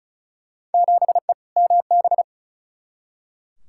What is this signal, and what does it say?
Morse code, heard as a keyed tone, about 36 wpm, 690 Hz. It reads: 7E MB
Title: Field Day Exchanges Encore - 35wpm